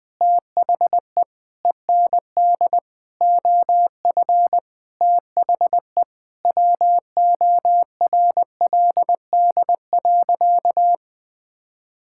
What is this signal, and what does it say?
Morse code, heard as a keyed tone, about 21 wpm, 700 Hz. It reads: THE END OF THE WORLD.
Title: Sentences with 5-Word Phrases - 20wpm